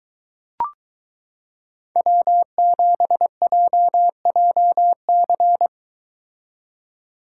Morse code, heard as a keyed tone, about 22 wpm, 700 Hz. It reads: W7JJC